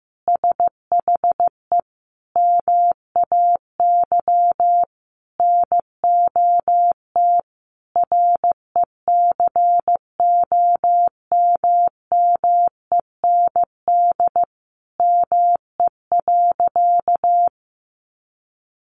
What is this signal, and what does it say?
Morse code, heard as a keyed tone, about 15 wpm, 700 Hz. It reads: SHE MAY NOT RECOMMEND ME.
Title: Sentences from Top 1200 Words 15wpm